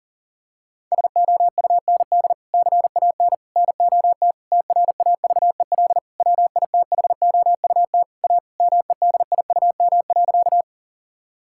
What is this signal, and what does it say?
Morse code, heard as a keyed tone, about 40 wpm, 700 Hz. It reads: SOUND CAN NOT TRAVEL WITHOUT A MEDIUM.
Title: Sentences from Top 1000 Words - Review Part 6 - 40wpm